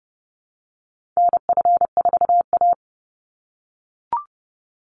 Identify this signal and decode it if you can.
Morse code, heard as a keyed tone, about 30 wpm, 700 Hz. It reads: NF4A
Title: Call Signs - 30wpm